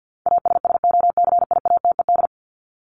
Morse code, heard as a keyed tone, about 50 wpm, 700 Hz. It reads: ASSOCIATED
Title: ED Suffix - Top 100 Words with Base Words 50wpm